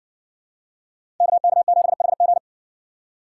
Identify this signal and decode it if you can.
Morse code, heard as a keyed tone, about 58 wpm, 700 Hz. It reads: KK7HG